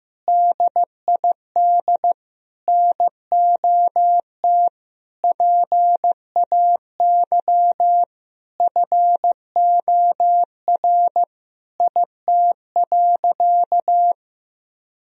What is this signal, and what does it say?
Morse code, heard as a keyed tone, about 16 wpm, 700 Hz. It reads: DID NOT PAY FOR IT.
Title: Sentences from Top 1000 Words - Review Part 3 - 15wpm